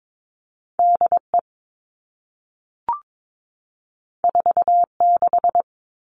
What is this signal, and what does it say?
Morse code, heard as a keyed tone, about 22 wpm, 700 Hz. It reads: DE 46
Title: All Cut Numbers - 2-Digit 22wpm